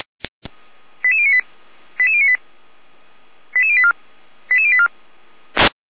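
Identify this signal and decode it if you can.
Five-tone selective call (ZVEI), heard as quick runs of five tones, tones of about 80 ms. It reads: 88098 88098 88084 88084
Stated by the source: ZVEI five-tone call 88078, received by SDR.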